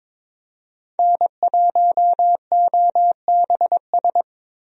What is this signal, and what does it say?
Morse code, heard as a keyed tone, about 22 wpm, 700 Hz. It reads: N1OBS
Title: USA Call Signs - 5 Characters - Repeated 3x - 22wpm